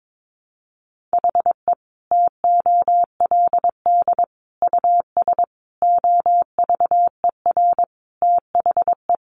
Morse code, heard as a keyed tone, about 22 wpm, 700 Hz. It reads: HE TOLD US OVER THE
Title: Sentences from Top 1100 Words 22wpm